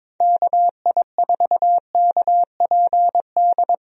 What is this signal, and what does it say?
Morse code, heard as a keyed tone, about 22 wpm, 700 Hz. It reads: KI4KPD
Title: USA Call Signs - 6 Characters - 22wpm